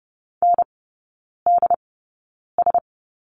Morse code, heard as a keyed tone, about 30 wpm, spaced as for 9 wpm, 700 Hz. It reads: NDS
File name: 081 Lesson 14e - Number 2 - Three Characters 30wpm